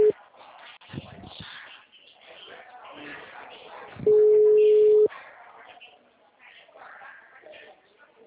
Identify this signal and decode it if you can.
Ringback tone (Europe, 425 Hz), with speech.